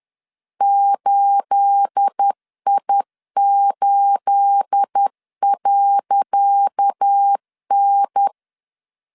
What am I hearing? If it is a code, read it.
Morse code, heard as a keyed tone, about 10 wpm, 800 Hz. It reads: 8I8.N
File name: LEZIONI / 10 SIGNS / 10C DOT 10wpm 2940